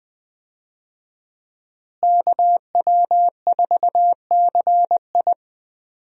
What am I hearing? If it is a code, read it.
Morse code, heard as a keyed tone, about 20 wpm, 700 Hz. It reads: KW4CI
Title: USA Call Signs - 5 Characters - Repeated 2x - 20wpm